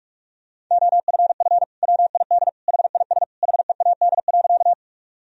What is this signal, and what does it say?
Morse code, heard as a keyed tone, about 47 wpm, 700 Hz. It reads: OFF WID HIS HEAD.
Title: Sentences from Top 300 Words and Abbreviations 45wpm